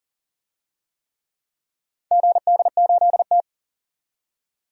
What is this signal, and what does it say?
Morse code, heard as a keyed tone, about 40 wpm, 700 Hz. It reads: GD8T
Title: DXCC-100 Call Signs - 4 Characters - 40wpm